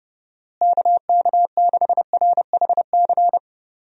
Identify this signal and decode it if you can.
Morse code, heard as a keyed tone, about 30 wpm, 700 Hz. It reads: KK6RHC
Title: USA Call Signs - 6 Characters - Repeated 3x - 30wpm